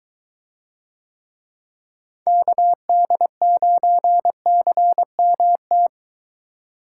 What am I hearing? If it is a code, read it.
Morse code, heard as a keyed tone, about 23 wpm, 700 Hz. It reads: KD9CMT